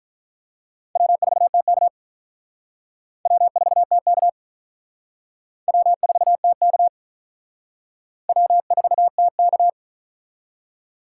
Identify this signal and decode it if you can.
Morse code, heard as a keyed tone, about 45 wpm, 700 Hz. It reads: W4TK W4TK W4TK W4TK